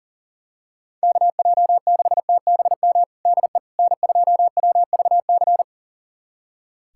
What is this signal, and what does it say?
Morse code, heard as a keyed tone, about 40 wpm, 700 Hz. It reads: KJ6TBM DE N2WVC
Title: USA Callsign DE Callsign 40wpm